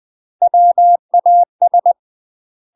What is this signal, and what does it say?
Morse code, heard as a keyed tone, about 20 wpm, 700 Hz. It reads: WAS
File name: UK MASTER QSOs TRAINING / 20WPM / UK MASTER QSOs 0058